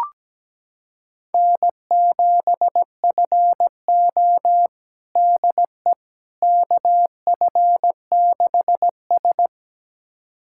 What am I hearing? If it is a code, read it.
Morse code, heard as a keyed tone, about 17 wpm, 700 Hz. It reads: N7FO DE KF6S
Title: USA Callsign DE Callsign - 4 Characters - 17wpm